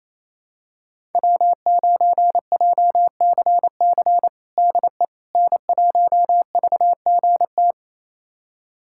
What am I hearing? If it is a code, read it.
Morse code, heard as a keyed tone, about 27 wpm, 700 Hz. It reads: W9JCC DE N1VGT
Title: USA Callsign DE Callsign - 5 Characters - 28wpm